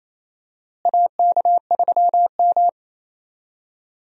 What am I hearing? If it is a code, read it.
Morse code, heard as a keyed tone, about 28 wpm, 700 Hz. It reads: AK3M
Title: USA Call Signs - 4 Characters - Repeated 2x - 28wpm